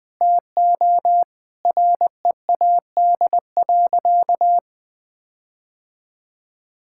Morse code, heard as a keyed tone, about 20 wpm, 700 Hz. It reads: TO READ.
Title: Sentences with 4-Word Phrases - Spoken Beginning - 20wpm